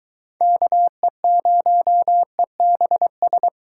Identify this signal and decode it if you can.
Morse code, heard as a keyed tone, about 23 wpm, 700 Hz. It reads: KE0EBS